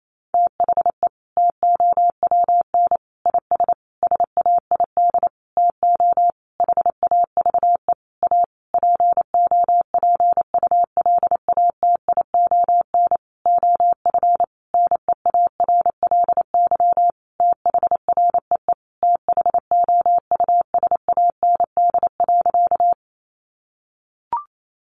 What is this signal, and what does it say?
Morse code, heard as a keyed tone, about 28 wpm, 700 Hz. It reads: THE TOWN IS SAID TO HAVE A POPULATION OF NEARLY THREE THOUSAND.
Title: Sentences from Top 1300 Words 28wpm